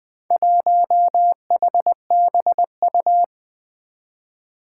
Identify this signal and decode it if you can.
Morse code, heard as a keyed tone, about 20 wpm, 700 Hz. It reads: 1HBU
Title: Four Challenging Letter-Number Combinations 20wpm